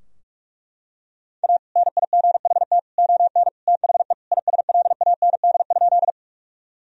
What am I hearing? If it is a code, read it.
Morse code, heard as a keyed tone, about 42 wpm, 700 Hz. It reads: A NIGHT ON THE ISLAND?
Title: Sentences from Top 1000 Words - Review Part 3 - 45wpm